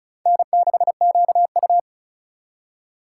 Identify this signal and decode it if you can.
Morse code, heard as a keyed tone, about 36 wpm, 700 Hz. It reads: N6QU
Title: USA Call Signs - 4 Characters - 3x Character Spacing - 35wpm